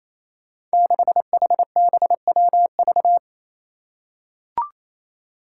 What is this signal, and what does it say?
Morse code, heard as a keyed tone, about 28 wpm, 700 Hz. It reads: 6HBWV
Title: Five Challenging Letter-Number Combinations 28wpm